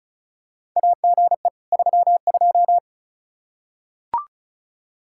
Morse code, heard as a keyed tone, about 36 wpm, 700 Hz. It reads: AGE 32
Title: Practice QSO - Exchange 3 - More Info - 35wpm